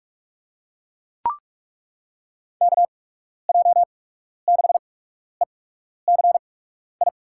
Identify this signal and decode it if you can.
Morse code, heard as a keyed tone, about 47 wpm, spaced as for 12 wpm, 700 Hz. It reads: KJ6ECI
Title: USA Call Signs - 6 Characters - 8x Character Spacing - 45wpm